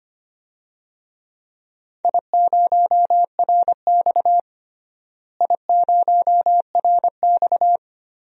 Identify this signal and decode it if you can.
Morse code, heard as a keyed tone, about 25 wpm, 700 Hz. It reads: I0RX I0RX
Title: DXCC-100 Call Signs - 4 Characters - Repeated 2x - 25wpm